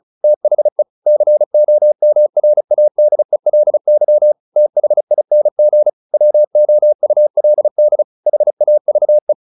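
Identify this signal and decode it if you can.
Morse code, heard as a keyed tone, about 34 wpm, 600 Hz. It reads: THE COMRADELY THING WOULD HAVE
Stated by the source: Morse practice at 35 wpm from CW med Gnister, sending Ken Follett's novel Never in English.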